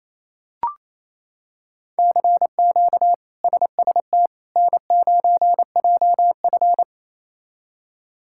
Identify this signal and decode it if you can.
Morse code, heard as a keyed tone, about 28 wpm, 700 Hz. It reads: CQ SST N9JF
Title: K1USN SST QSO Practice - 28wpm